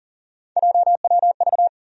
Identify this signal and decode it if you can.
Morse code, heard as a keyed tone, about 40 wpm, 700 Hz. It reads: JWV
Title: USA Call Sign Postfixes - 3 Letters 40wpm